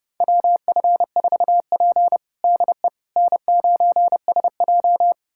Morse code, heard as a keyed tone, about 30 wpm, 700 Hz. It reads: WF4P DE N9SJ
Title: USA Callsign DE Callsign - 4 Characters - 30wpm